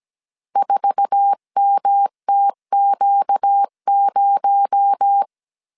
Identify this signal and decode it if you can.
Morse code, heard as a keyed tone, about 17 wpm, 800 Hz. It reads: 4MTQ0
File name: CW / DET / 1 9 1500C S C 16wpm 2909